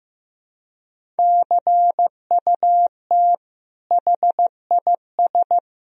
Morse code, heard as a keyed tone, about 16 wpm, 700 Hz. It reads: CUT HIS
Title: Sentences from Top 1000 Words - Review Part 1 - 15wpm